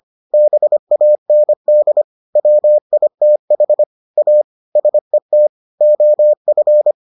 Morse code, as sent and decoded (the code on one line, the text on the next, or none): -... .- -. -.. / .-- .. - .... / .- / ... - / --- ..-.
BAND WITH A SET OF